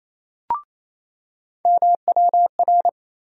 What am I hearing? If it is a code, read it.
Morse code, heard as a keyed tone, about 27 wpm, 700 Hz. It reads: MWR
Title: USA Call Sign Postfixes - 3 Letters 28wpm